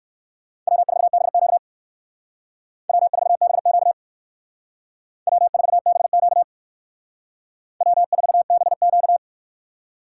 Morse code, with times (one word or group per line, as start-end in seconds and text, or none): 0.7-1.6 W4BQ
2.9-3.9 W4BQ
5.3-6.4 W4BQ
7.8-9.2 W4BQ